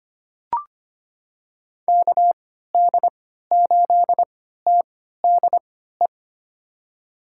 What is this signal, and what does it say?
Morse code, heard as a keyed tone, about 25 wpm, spaced as for 14 wpm, 700 Hz. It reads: KD8TDE
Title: USA Call Signs - 6 Characters - 3x Character Spacing - 25wpm